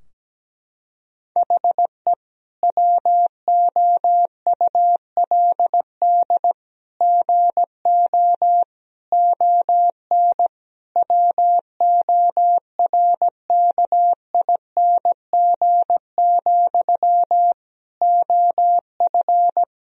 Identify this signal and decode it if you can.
Morse code, heard as a keyed tone, about 17 wpm, 700 Hz. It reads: HE WOULD GO ON WORKING, OF
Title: Sentences from Top 1000 Words - Review Part 4 - 17wpm